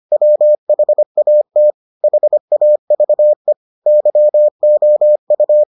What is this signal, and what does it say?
Morse code, heard as a keyed tone, about 25 wpm, 600 Hz. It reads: WHAT HAVE YOU